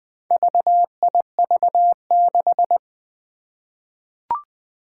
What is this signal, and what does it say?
Morse code, heard as a keyed tone, about 20 wpm, 700 Hz. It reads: VIV6
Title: Four Challenging Letter-Number Combinations 20wpm